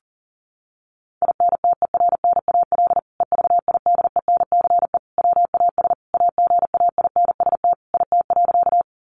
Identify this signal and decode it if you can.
Morse code, heard as a keyed tone, about 40 wpm, 700 Hz. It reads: INTERNAL EVIDENCE WAS AGAINST IT.
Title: Sentences from Top 1400 Words 40wpm